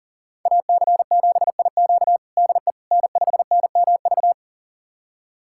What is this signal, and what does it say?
Morse code, heard as a keyed tone, about 37 wpm, 700 Hz. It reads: AC7IQ DE N5NMV
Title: USA Callsign DE Callsign - 5 Characters - 40wpm